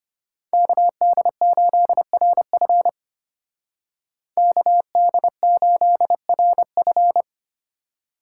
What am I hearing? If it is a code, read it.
Morse code, heard as a keyed tone, about 33 wpm, 700 Hz. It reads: KD8RF KD8RF